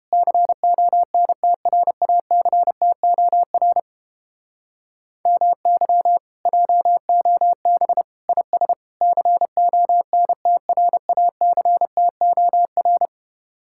Morse code, heard as a keyed tone, about 29 wpm, 700 Hz. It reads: CONTRACTOR MY JOB IS CONTRACTOR